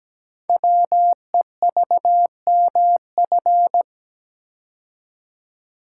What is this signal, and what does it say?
Morse code, heard as a keyed tone, about 17 wpm, 700 Hz. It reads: WEVMF
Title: Code Groups - 5 Letters - 17wpm